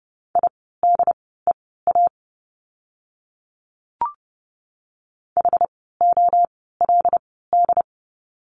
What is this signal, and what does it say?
Morse code, heard as a keyed tone, about 30 wpm, spaced as for 17 wpm, 700 Hz. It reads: IDEA HOLD
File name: course / 30:10wpm / 108 Lesson 18f - Letter m - Words 30wpm